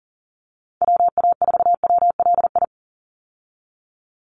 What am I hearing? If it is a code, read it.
Morse code, heard as a keyed tone, about 40 wpm, 700 Hz. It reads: WA4WLI
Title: Call Signs - 40wpm